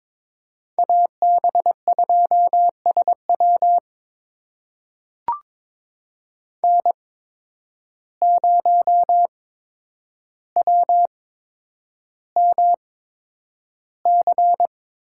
Morse code, heard as a keyed tone, about 22 wpm, spaced as for 9 wpm, 700 Hz. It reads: AB2SW N0WMC
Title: USA Call Signs - 5 Characters - 8x Character Spacing - 22wpm